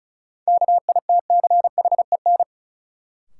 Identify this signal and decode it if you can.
Morse code, heard as a keyed tone, about 36 wpm, 700 Hz. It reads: KITCHEN
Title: Sentences from Top 1400 Words - New Word Drill - 35wpm